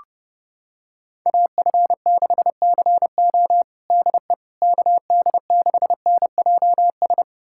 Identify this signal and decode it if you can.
Morse code, heard as a keyed tone, about 30 wpm, 700 Hz. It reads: AF6CO DE KD6NJS
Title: USA Callsign DE Callsign 30wpm